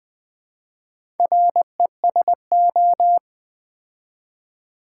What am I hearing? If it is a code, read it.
Morse code, heard as a keyed tone, about 20 wpm, 700 Hz. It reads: RESO
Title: Build A Word - 4 Letters At A Time - 10-Letter Words - 20wpm